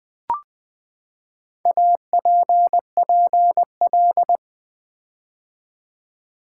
Morse code, heard as a keyed tone, about 20 wpm, 700 Hz. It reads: APPL
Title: Build A Word - 4 Letters At A Time - 11-Letter Words - 20wpm